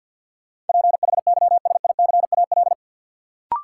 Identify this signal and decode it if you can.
Morse code, heard as a keyed tone, about 51 wpm, 700 Hz. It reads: PHYSICAL
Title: Sentences from Top 1400 Words - New Word Drill Alt - 50wpm